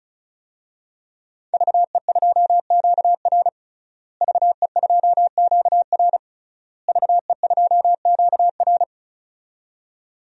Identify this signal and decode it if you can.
Morse code, heard as a keyed tone, about 36 wpm, 700 Hz. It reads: VE2QR VE2QR VE2QR
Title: DXCC-100 Call Signs - 5 Characters - Repeated 3x - 35wpm